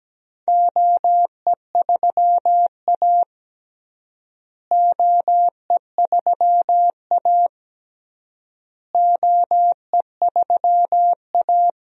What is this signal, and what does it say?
Morse code, heard as a keyed tone, about 17 wpm, 700 Hz. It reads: OE3A OE3A OE3A
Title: DXCC-100 Call Signs - 4 Characters - Repeated 3x - 17wpm